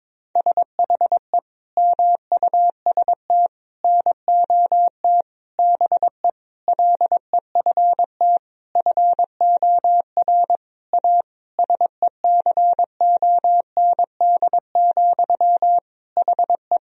Morse code, heard as a keyed tone, about 21 wpm, 700 Hz. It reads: SHE MUST NOT BE LEFT FOR A SECOND, HE
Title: Sentences from Top 1000 Words - Review Part 1 - 22wpm